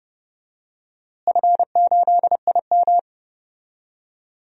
Morse code, heard as a keyed tone, about 30 wpm, 700 Hz. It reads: F8IM
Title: DXCC-100 Call Signs - 4 Characters - Repeated 3x - 30wpm